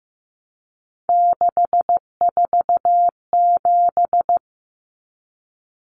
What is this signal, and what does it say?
Morse code, heard as a keyed tone, about 15 wpm, 700 Hz. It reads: 647
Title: Numbers - 3-Digit 15wpm